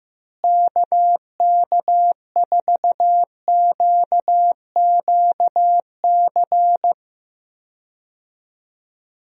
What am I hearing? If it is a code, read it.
Morse code, heard as a keyed tone, about 15 wpm, 700 Hz. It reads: KK4QQC
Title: USA Call Signs - 6 Characters - Repeated 3x - 15wpm